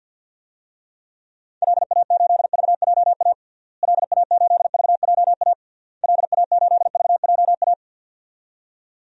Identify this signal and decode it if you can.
Morse code, heard as a keyed tone, about 51 wpm, 700 Hz. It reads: LA8VJA LA8VJA LA8VJA
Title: DXCC-100 Call Signs - 6 Characters - Repeated 3x - 50wpm